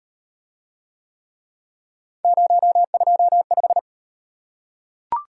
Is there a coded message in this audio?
Yes: Morse code from 2.2 to 3.8 s, heard as a keyed tone, about 38 wpm, 700 Hz. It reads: 025